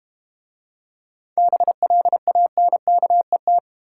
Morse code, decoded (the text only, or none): BLANKET